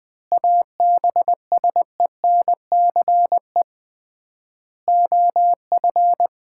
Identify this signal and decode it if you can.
Morse code, heard as a keyed tone, about 21 wpm, 700 Hz. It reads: ABSENCE OF
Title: 3x Word-Spacing - Sets of 4 Words - 20wpm